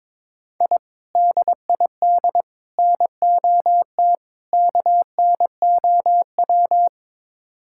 Morse code, heard as a keyed tone, about 22 wpm, 700 Hz. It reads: I DID NOT KNOW
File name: practice / 22wpm / Build A Phrase - 5-Word Phrases - 22wpm